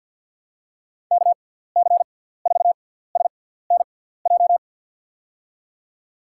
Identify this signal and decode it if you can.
Morse code, heard as a keyed tone, about 47 wpm, spaced as for 17 wpm, 700 Hz. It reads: KC4SNJ
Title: USA Call Signs - 6 Characters - 6x Character Spacing - 50wpm